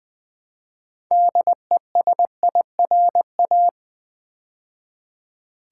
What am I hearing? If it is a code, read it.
Morse code, heard as a keyed tone, about 20 wpm, 700 Hz. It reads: DESIRA
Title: Build A Word - 3 Letters At A Time - 9-Letter Words - 20wpm